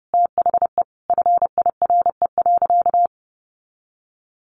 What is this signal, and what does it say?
Morse code, heard as a keyed tone, about 30 wpm, 700 Hz. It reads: THE FIRE.
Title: Sentences from Top 1400 Words 30wpm